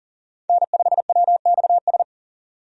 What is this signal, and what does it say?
Morse code, heard as a keyed tone, about 40 wpm, 700 Hz. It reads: N5WXS
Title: USA Call Signs - 5 Characters - 40wpm